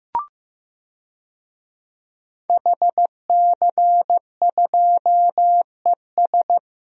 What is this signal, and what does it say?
Morse code, heard as a keyed tone, about 15 wpm, 700 Hz. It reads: HC2ES
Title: DXCC-100 Call Signs - 5 Characters - Repeated 3x - 15wpm